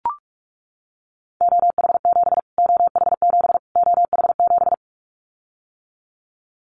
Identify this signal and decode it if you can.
Morse code, heard as a keyed tone, about 44 wpm, 700 Hz. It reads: OH7 OH7 OH7
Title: Callsign Prefixes for Top 25 DXCC - Repeated 3x - 45wpm